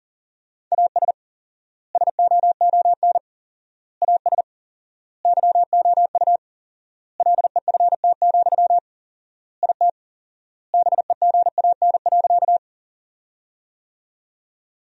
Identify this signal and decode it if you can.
Morse code, heard as a keyed tone, about 40 wpm, 700 Hz. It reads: AS SOON AS YOU LEFT, IT BEGAN.
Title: 4x Word-Spacing - Sentences from Top 400 Words - 40wpm